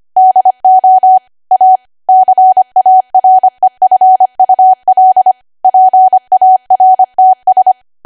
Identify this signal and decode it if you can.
Morse code, heard as a keyed tone, about 25 wpm, 750 Hz. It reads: DO A CAREFUL PARTS